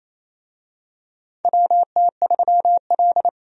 Morse code, heard as a keyed tone, about 28 wpm, 700 Hz. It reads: WT3L
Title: USA Call Signs - 4 Characters - 28wpm